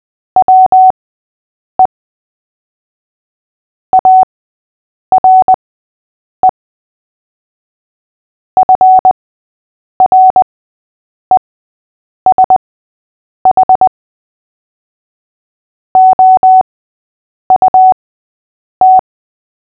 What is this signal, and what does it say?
Morse code, heard as a keyed tone, about 20 wpm, spaced as for 8 wpm, 750 Hz. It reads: WE ARE FRESH OUT
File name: cw / fists / Short-Story-8-8wpm